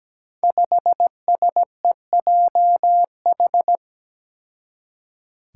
Morse code, heard as a keyed tone, about 17 wpm, 700 Hz. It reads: 5SEJH
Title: Five Challenging Letter-Number Combinations 17wpm